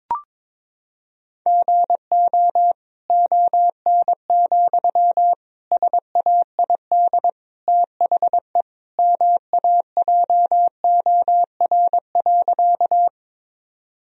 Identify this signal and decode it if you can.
Morse code, heard as a keyed tone, about 22 wpm, 700 Hz. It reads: GO ON, SAID THE MAJOR.
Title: Sentences from Top 1000 Words - Review Part 4 - 22wpm